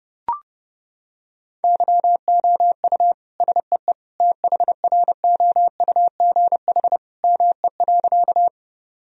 Morse code, heard as a keyed tone, about 32 wpm, 700 Hz. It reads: YOU SEE THROUGH ME.
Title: Sentences from Top 1000 Words - Review Part 1 - 30wpm